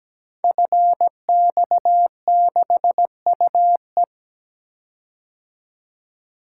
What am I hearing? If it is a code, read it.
Morse code, heard as a keyed tone, about 17 wpm, 700 Hz. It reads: FX6UE
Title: Code Groups - 5 Letter:Numbers - 17wpm